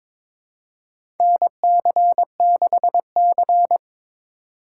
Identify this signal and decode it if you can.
Morse code, heard as a keyed tone, about 22 wpm, 700 Hz. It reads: NC6C
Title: USA Call Signs - 4 Characters - 2x Character Spacing - 22wpm